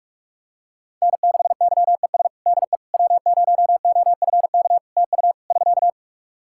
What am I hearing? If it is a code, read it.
Morse code, heard as a keyed tone, about 46 wpm, 700 Hz. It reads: N6YES DE W0OFK TU <SK>
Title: K1USN SST QSO Practice 45wpm